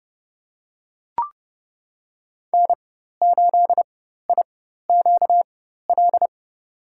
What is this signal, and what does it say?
Morse code, heard as a keyed tone, about 30 wpm, spaced as for 14 wpm, 700 Hz. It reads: N8IQL
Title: USA Call Signs - 4x Character Spacing - 30wpm